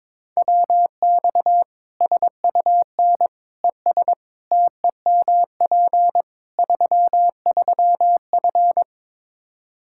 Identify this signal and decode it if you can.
Morse code, heard as a keyed tone, about 22 wpm, 700 Hz. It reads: WX SUN ES TEMP 33F
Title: K1USN SST QSO Practice 22wpm